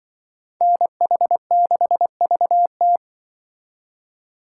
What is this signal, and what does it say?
Morse code, heard as a keyed tone, about 25 wpm, 700 Hz. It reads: NH6VT
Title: Speed Racing - USA 2x2 Call Signs 20wpm